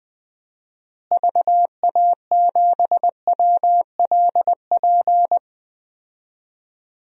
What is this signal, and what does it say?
Morse code, heard as a keyed tone, about 20 wpm, 700 Hz. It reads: VA7WLP
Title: DXCC-100 Call Signs - 6 Characters - Repeated 2x - 20wpm